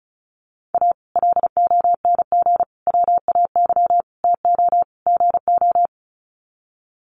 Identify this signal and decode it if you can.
Morse code, heard as a keyed tone, about 34 wpm, 700 Hz. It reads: A LONG WAY TO GO
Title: Sets of 5 Words 35wpm